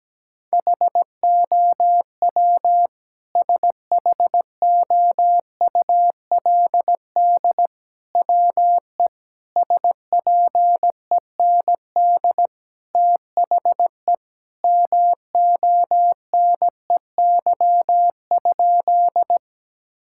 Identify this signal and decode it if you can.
Morse code, heard as a keyed tone, about 17 wpm, 700 Hz. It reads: HOW SHOULD WE SPEND THE MONEY?
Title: Sentences from Top 1000 Words - Review Part 5 - 17wpm